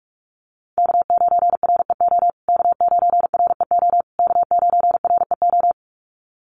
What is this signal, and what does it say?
Morse code, heard as a keyed tone, about 44 wpm, 700 Hz. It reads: K9REO K9REO K9REO
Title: USA Call Signs Repeated 3x 45wpm